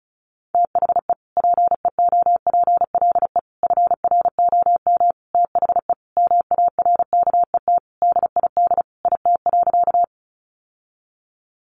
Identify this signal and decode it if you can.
Morse code, heard as a keyed tone, about 34 wpm, 700 Hz. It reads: THE PEOPLE FROM THE MARKET DID IT.
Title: Sentences from Top 600 Words 35wpm